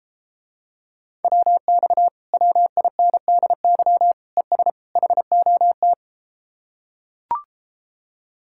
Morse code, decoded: WX WINDY ES HOT

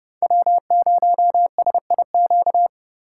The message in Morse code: W0SIQ

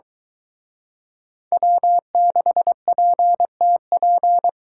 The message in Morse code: W6PTP

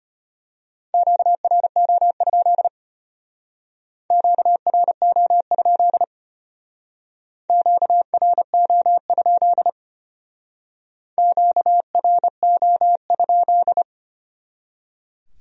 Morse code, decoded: QRO? QRO? QRO? QRO?

